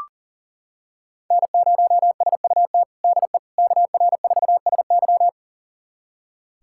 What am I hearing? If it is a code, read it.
Morse code, heard as a keyed tone, about 37 wpm, 700 Hz. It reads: N0SUT DE KR4SY
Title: USA Callsign DE Callsign - 5 Characters - 40wpm